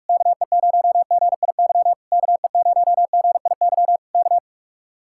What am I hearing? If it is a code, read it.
Morse code, heard as a keyed tone, about 44 wpm, 700 Hz. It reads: KE0GIY KE0GIY K